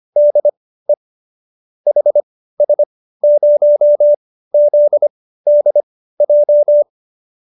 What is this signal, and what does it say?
Morse code, heard as a keyed tone, about 25 wpm, spaced as for 15 wpm, 590 Hz. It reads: DE HS0ZDJ